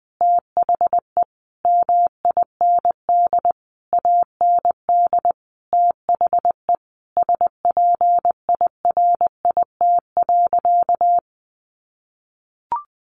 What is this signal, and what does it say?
Morse code, heard as a keyed tone, about 20 wpm, 700 Hz. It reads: THE MIND AND THE SPIRIT.
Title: Sentences from Top 1400 Words 20wpm